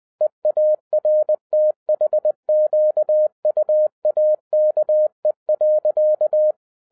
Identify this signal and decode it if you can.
Morse code, heard as a keyed tone, about 20 wpm, 610 Hz. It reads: EARTHQUAKE.
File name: 2011-10-mp3 / n03-TARO JR0QWW 20wpm